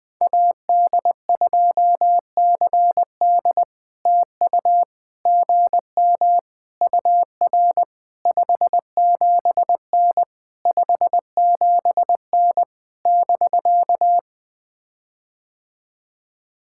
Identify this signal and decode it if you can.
Morse code, heard as a keyed tone, about 20 wpm, 700 Hz. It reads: AD2CD TU GM UR 57N 57N <BK>
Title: POTA QSO Practice - 20wpm